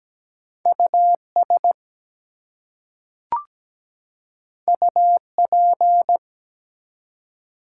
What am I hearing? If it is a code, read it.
Morse code, heard as a keyed tone, about 17 wpm, 700 Hz. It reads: US UP
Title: Two Letter Words - 17wpm